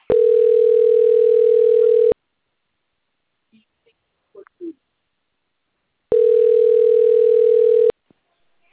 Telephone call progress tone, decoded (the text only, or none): ringback tone